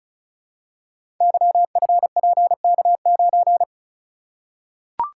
Morse code, heard as a keyed tone, about 36 wpm, 700 Hz. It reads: YFPK9
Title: Code Groups - 5 Letter:Numbers - 35wpm